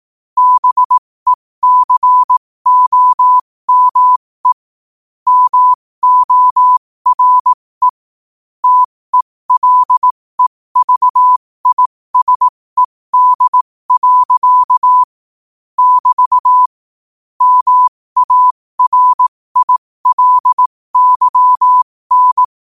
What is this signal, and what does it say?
Morse code, heard as a keyed tone, about 18 wpm, spaced as for 15 wpm, 1000 Hz. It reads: BECOME MORE TELEVISED. <BT> MARILYN
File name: qotdfiles / 15wpm